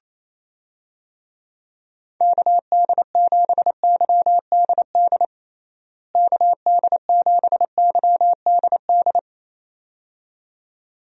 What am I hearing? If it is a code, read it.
Morse code, heard as a keyed tone, about 28 wpm, 700 Hz. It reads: KD7YDD KD7YDD
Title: USA Call Signs - 6 Characters - Repeated 2x - 28wpm